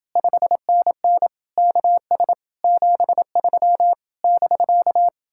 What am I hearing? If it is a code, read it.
Morse code, heard as a keyed tone, about 30 wpm, 700 Hz. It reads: 5NN KS 73 <BK>